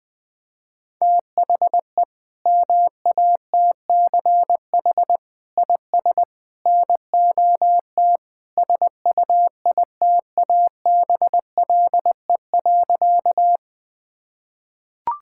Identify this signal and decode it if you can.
Morse code, heard as a keyed tone, about 20 wpm, 700 Hz. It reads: THE MATCH IS NOT SUITABLE.
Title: Sentences from Top 1600 Words 20wpm